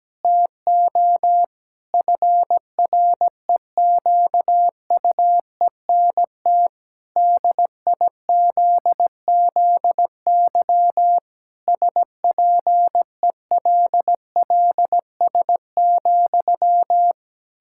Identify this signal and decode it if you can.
Morse code, heard as a keyed tone, about 17 wpm, 700 Hz. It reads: TO FREQUENT DIZZY SPELLS,
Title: Pangram Challenge v2 - 17wpm